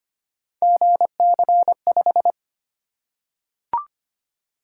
Morse code, heard as a keyed tone, about 25 wpm, 700 Hz. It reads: GC5
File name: Three Letter-Number Combinations - 25wpm